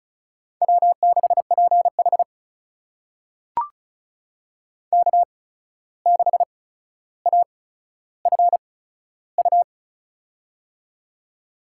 Morse code, heard as a keyed tone, about 34 wpm, spaced as for 14 wpm, 700 Hz. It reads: W6PH K6AFU
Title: USA Call Signs - 8x Character Spacing - 35wpm